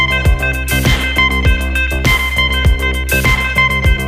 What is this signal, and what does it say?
Emergency Alert System (SAME) tones, heard as harsh buzzing bursts. It is heard from 1.0 to 3.8 s.